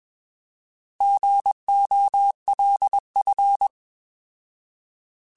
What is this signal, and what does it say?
Morse code, heard as a keyed tone, about 21 wpm, 800 Hz. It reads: GOLF
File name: Alphabetical / Alphabetical mp3s / Gs @20